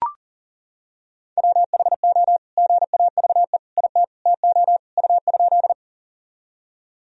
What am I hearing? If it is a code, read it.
Morse code, heard as a keyed tone, about 40 wpm, 700 Hz. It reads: WHO GAVE IT TO U?
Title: Sentences from Top 500 Words and Abbreviations 40wpm